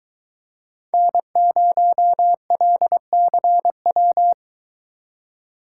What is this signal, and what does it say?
Morse code, heard as a keyed tone, about 23 wpm, 700 Hz. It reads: N0LCW